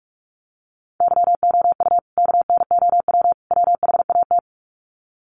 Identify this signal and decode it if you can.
Morse code, heard as a keyed tone, about 44 wpm, 700 Hz. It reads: YOU KNOW WHAT
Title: Sets of 3 Words 45wpm